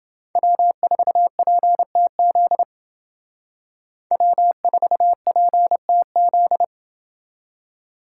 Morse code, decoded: W4PTZ W4PTZ

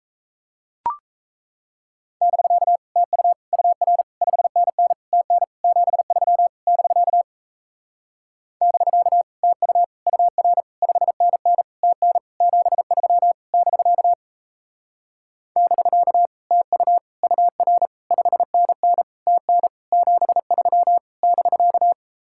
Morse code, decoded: <BK> TU UR 5NN TN 73 <BK> <BK> TU UR 5NN TN 73 <BK> <BK> TU UR 5NN TN 73 <BK>

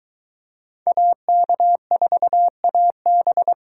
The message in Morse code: AK4AB

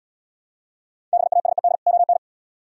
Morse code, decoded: BASED ON

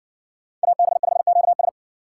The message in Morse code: AB5QS

